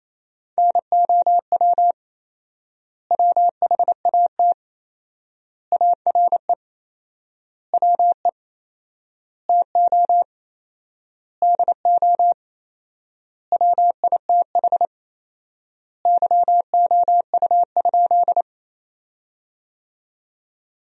Morse code, heard as a keyed tone, about 27 wpm, 700 Hz. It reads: NOW WHAT ARE WE TO DO WITH YOU?
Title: 4x Word-Spacing - Sentences from Top 100 Words - 28wpm